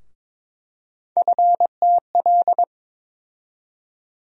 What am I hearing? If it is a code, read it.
Morse code, heard as a keyed tone, about 22 wpm, 700 Hz. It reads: FTL
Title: USA Call Sign Postfixes - 3 Letters 22wpm